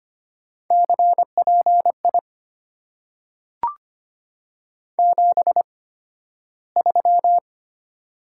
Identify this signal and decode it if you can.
Morse code, heard as a keyed tone, about 25 wpm, spaced as for 13 wpm, 700 Hz. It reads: CPI 73
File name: Top 90 QSO Elements - 8x Character Spacing - 25wpm